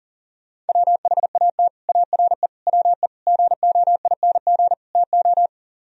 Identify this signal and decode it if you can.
Morse code, heard as a keyed tone, about 49 wpm, 700 Hz. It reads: WHAT ARE WE GOING TO
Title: Speed-Racing - Sets of 5 Words 40wpm